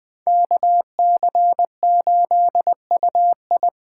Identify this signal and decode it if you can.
Morse code, heard as a keyed tone, about 24 wpm, 700 Hz. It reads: KC8UI
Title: Speed Racing - USA 2x2 Call Signs 20wpm